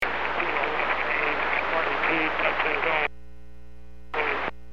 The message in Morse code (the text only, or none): none